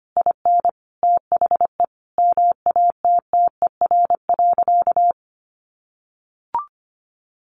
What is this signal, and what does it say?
Morse code, heard as a keyed tone, about 25 wpm, 700 Hz. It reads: IN THE MATTER.
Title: Sentences from Top 1200 Words 25wpm